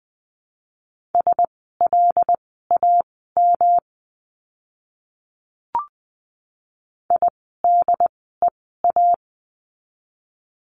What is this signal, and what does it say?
Morse code, heard as a keyed tone, about 20 wpm, spaced as for 15 wpm, 700 Hz. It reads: SLAM IDEA
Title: 108 Lesson 18f - Letter m - Words 20wpm